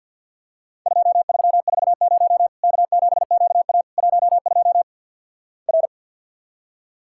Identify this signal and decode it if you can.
Morse code, heard as a keyed tone, about 50 wpm, 700 Hz. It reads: 2340 K7QA 12 R
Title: Worked All Europe DX Contest - QTC Exchange with Repeats - 50wpm